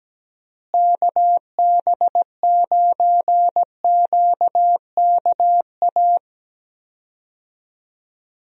Morse code, heard as a keyed tone, about 17 wpm, 700 Hz. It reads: KB9QKA
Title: USA Call Signs - 6 Characters - Repeated 2x - 17wpm